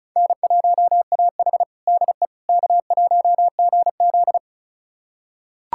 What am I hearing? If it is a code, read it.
Morse code, heard as a keyed tone, about 36 wpm, 700 Hz. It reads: N1AH DE K1GZ
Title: USA Callsign DE Callsign - 4 Characters - 35wpm